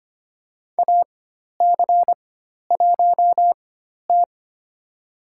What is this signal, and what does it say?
Morse code, heard as a keyed tone, about 25 wpm, spaced as for 12 wpm, 700 Hz. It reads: AC1T
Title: USA Call Signs - 4 Characters - 4x Character Spacing - 25wpm